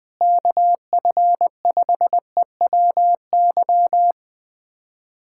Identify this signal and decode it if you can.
Morse code, heard as a keyed tone, about 20 wpm, 700 Hz. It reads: KF5EWY
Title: USA Callsign DE Callsign - 6 Characters - 20wpm